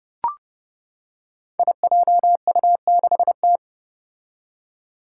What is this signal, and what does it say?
Morse code, heard as a keyed tone, about 30 wpm, 700 Hz. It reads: IJU6T
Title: Five Challenging Letter-Number Combinations 30wpm